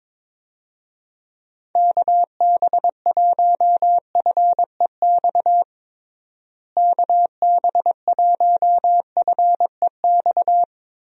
Morse code, heard as a keyed tone, about 22 wpm, 700 Hz. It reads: KB1FEX KB1FEX
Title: USA Call Signs - 6 Characters - Repeated 2x - 22wpm